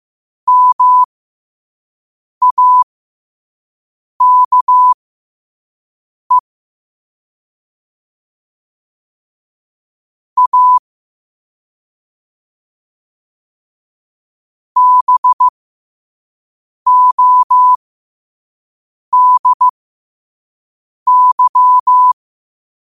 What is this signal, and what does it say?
Morse code, heard as a keyed tone, about 15 wpm, spaced as for 5 wpm, 1000 Hz. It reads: MAKE A BODY
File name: qotdfiles / 5wpm